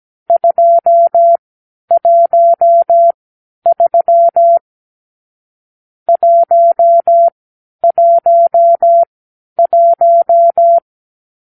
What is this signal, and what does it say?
Morse code, heard as a keyed tone, about 17 wpm, spaced as for 10 wpm, 680 Hz. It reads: 213 111
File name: k7qo / Lesson 44 (1-3)